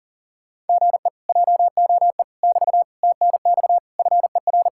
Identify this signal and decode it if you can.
Morse code, heard as a keyed tone, about 40 wpm, 700 Hz. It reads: GE JOE <BT> TNX FER